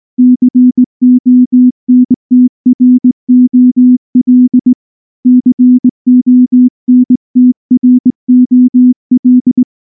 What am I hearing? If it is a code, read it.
Morse code, heard as a keyed tone, about 19 wpm, 250 Hz. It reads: CONTROL CONTROL
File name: morse / morse-training-20WPM-7